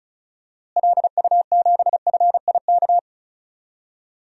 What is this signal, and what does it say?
Morse code, heard as a keyed tone, about 36 wpm, 700 Hz. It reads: LU7FIK
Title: DXCC-100 Call Signs - 6 Characters - Repeated 2x - 35wpm